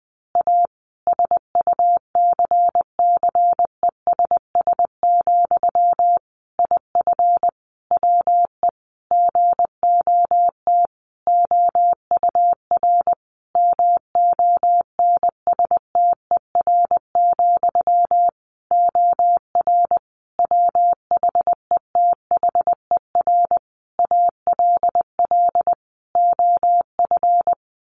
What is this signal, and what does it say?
Morse code, heard as a keyed tone, about 20 wpm, 700 Hz. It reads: A SUCCESS, IF WE GOT OUR MONSTER, OR WHETHER ALL OF
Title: A Sound Of Thunder 20wpm